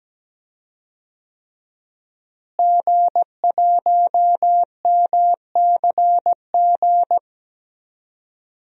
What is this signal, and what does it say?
Morse code, heard as a keyed tone, about 17 wpm, 700 Hz. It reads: G1MCG
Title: DXCC-100 Call Signs - 5 Characters - Repeated 3x - 17wpm